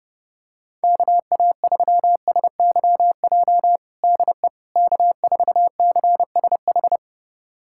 Morse code, heard as a keyed tone, about 30 wpm, 700 Hz. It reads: KA3SYJ DE K4CSH
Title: USA Callsign DE Callsign 30wpm